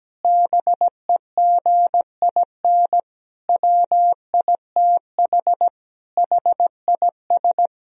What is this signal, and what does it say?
Morse code, heard as a keyed tone, about 17 wpm, 700 Hz. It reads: BEGIN WITH HIS
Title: Sentences from Top 1000 Words - Review Part 2 - 17wpm